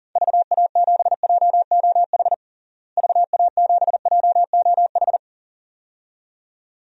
Morse code, heard as a keyed tone, about 40 wpm, 700 Hz. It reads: VA7JOH VA7JOH
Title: DXCC-100 Call Signs - 6 Characters - Repeated 2x - 40wpm